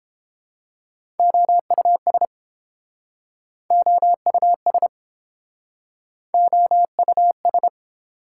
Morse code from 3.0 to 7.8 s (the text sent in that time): OUS OUS